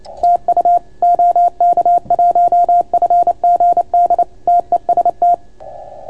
Morse code, heard as a keyed tone, about 29 wpm, 690 Hz. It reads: TU OK1FGD TEST